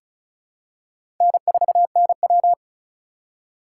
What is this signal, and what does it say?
Morse code, heard as a keyed tone, about 36 wpm, 700 Hz. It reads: N4NW
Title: K1USN SST QSO Practice - 35wpm